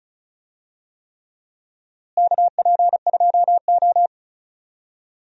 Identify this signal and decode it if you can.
Morse code, heard as a keyed tone, about 36 wpm, 700 Hz. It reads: KP2O
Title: DXCC-100 Call Signs - 4 Characters - Repeated 3x - 35wpm